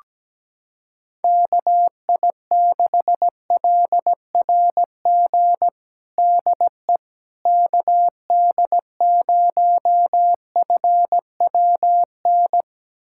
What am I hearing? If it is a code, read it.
Morse code, heard as a keyed tone, about 17 wpm, 700 Hz. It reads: KI6LRG DE KD0FWN
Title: USA Callsign DE Callsign - 6 Characters - 17wpm